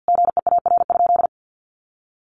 Morse code, heard as a keyed tone, about 50 wpm, 700 Hz. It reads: GEAR?
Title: Sentences from Top 1400 Words 50wpm